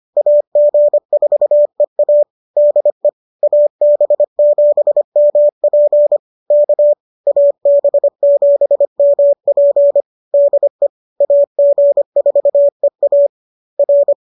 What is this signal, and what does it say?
Morse code, heard as a keyed tone, about 25 wpm, 590 Hz. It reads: AG4EA DE AB7MP K AB7MP DE AG4EA R
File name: qso206 25